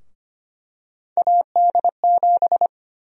Morse code, heard as a keyed tone, about 25 wpm, 700 Hz. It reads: AD7
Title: USA Call Sign Prefixes - 25wpm